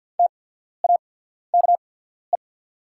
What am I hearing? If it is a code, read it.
Morse code, heard as a keyed tone, about 49 wpm, spaced as for 14 wpm, 690 Hz. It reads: TAKE
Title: 8x Character Spacing - Top 300 Words - 50wpm